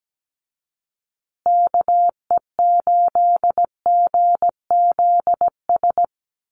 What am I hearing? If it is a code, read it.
Morse code, heard as a keyed tone, about 17 wpm, 700 Hz. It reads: KE8GZS
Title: USA Call Signs Repeated 3x 17wpm